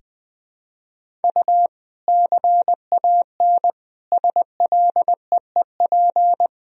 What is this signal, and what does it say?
Morse code, heard as a keyed tone, about 20 wpm, 700 Hz. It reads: U CAN SLEEP
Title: Sentences from Top 500 Words and Abbreviations 20wpm